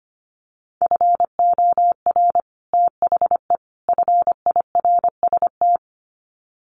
Morse code, heard as a keyed tone, about 25 wpm, 700 Hz. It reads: FOR THE FIRST
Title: Sets of 3 Words 25wpm